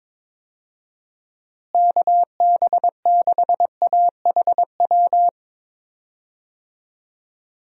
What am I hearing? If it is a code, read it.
Morse code, heard as a keyed tone, about 21 wpm, 700 Hz. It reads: KB6AHW
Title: USA Call Signs - 6 Characters - Repeated 3x - 22wpm